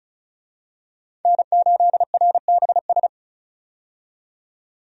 Morse code, heard as a keyed tone, about 36 wpm, 700 Hz. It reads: N8RBS
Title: USA Call Signs - 5 Characters - 35wpm